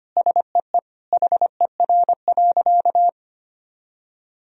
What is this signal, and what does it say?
Morse code, heard as a keyed tone, about 25 wpm, 700 Hz. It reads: SEE HER.
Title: Sentences from Top 1000 Words - Review Part 4 - 25wpm